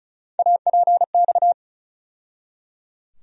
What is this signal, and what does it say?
Morse code, heard as a keyed tone, about 34 wpm, 700 Hz. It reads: APX
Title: USA Call Sign Postfixes - 3 Letters 35wpm